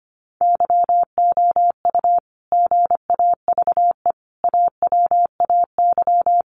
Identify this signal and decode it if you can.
Morse code, heard as a keyed tone, about 25 wpm, 700 Hz. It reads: YOU GAVE AWAY
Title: Sentences from Top 1300 Words 25wpm